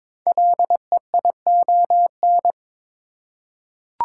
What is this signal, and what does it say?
Morse code, heard as a keyed tone, about 22 wpm, 700 Hz. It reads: LEION
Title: Code Groups - 5 Letter:Numbers - 22wpm